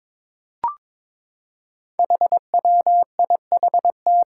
Morse code, heard as a keyed tone, about 22 wpm, 700 Hz. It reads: HWIHT
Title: Five Challenging Letter-Number Combinations 22wpm